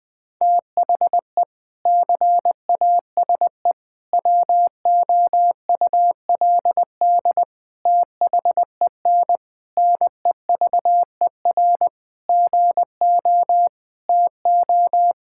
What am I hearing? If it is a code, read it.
Morse code, heard as a keyed tone, about 20 wpm, 700 Hz. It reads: THE CASE WOULD THEN NEVER GO TO
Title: Sentences from Top 1600 Words 20wpm